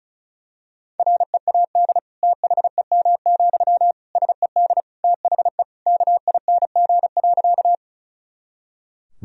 Morse code, heard as a keyed tone, about 36 wpm, 700 Hz. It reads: READ THEM, SED THE KING.
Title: Sentences from Top 400 Words and Abbreviations 35wpm